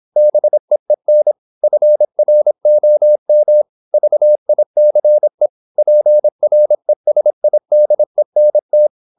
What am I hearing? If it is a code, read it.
Morse code, heard as a keyed tone, about 26 wpm, 600 Hz. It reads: BEEN FROM VICE PRESIDENT
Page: Never 26wpm | CW med Gnister